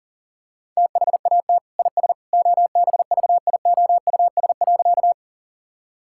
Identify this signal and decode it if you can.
Morse code, heard as a keyed tone, about 40 wpm, 700 Hz. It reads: THAT IS OBVIOUS.
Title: Sentences from Top 1600 Words 40wpm